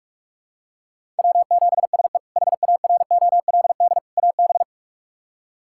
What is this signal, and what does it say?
Morse code, heard as a keyed tone, about 41 wpm, 700 Hz. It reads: W7SE HAROLD AB
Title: K1USN SST QSO Practice - 45wpm